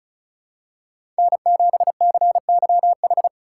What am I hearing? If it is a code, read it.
Morse code, heard as a keyed tone, about 34 wpm, 700 Hz. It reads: N7CYH
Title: USA Call Signs - 5 Characters - 3x Character Spacing - 35wpm